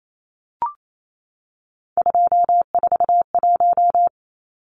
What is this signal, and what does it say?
Morse code, heard as a keyed tone, about 28 wpm, 700 Hz. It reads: 241
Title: Numbers - 3-Digit 28wpm